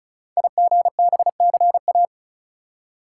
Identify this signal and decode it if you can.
Morse code, heard as a keyed tone, about 36 wpm, 700 Hz. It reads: IGBCA
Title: Code Groups - 5 Letters - 35wpm